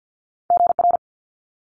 Morse code, heard as a keyed tone, about 50 wpm, 700 Hz. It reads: GR
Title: Multi-letter Phonemes Quick Fire 50wpm